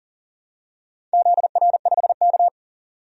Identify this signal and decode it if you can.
Morse code, heard as a keyed tone, about 38 wpm, 690 Hz. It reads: ZR5K
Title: DXCC-100 Call Signs - 4 Characters - 40wpm